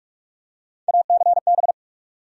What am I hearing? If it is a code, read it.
Morse code, heard as a keyed tone, about 46 wpm, 700 Hz. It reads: ACB
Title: USA Call Sign Postfixes - 3 Letters 45wpm